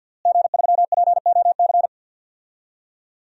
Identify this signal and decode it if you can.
Morse code, heard as a keyed tone, about 53 wpm, 700 Hz. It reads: G3POC
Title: DXCC-100 Call Signs - 5 Characters - Repeated 3x - 50wpm